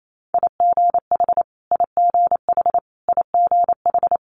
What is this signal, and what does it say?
Morse code, heard as a keyed tone, about 28 wpm, 700 Hz. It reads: IGH IGH IGH
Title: Multi-letter Phonemes Repeated 3x 28wpm